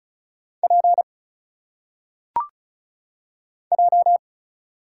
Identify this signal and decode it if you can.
Morse code, heard as a keyed tone, about 34 wpm, 700 Hz. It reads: P J
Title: Single Letters - Phonetic Pronunciation 35wpm